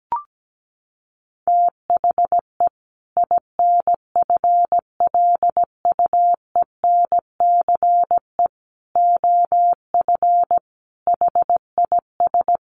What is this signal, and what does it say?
Morse code, heard as a keyed tone, about 17 wpm, 700 Hz. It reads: THE INFLUENCE OF HIS
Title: Sentences from Top 1100 Words 17wpm